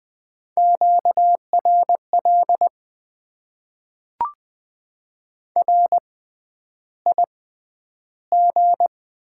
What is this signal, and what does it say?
Morse code, heard as a keyed tone, about 20 wpm, spaced as for 10 wpm, 700 Hz. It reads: QRL RIG
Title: Top 20 QSO Elements - 6x Character Spacing - 20wpm